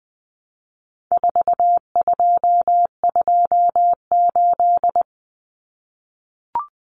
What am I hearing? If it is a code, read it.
Morse code, heard as a keyed tone, about 20 wpm, 700 Hz. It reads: 4228
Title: Numbers - 4-Digit 20wpm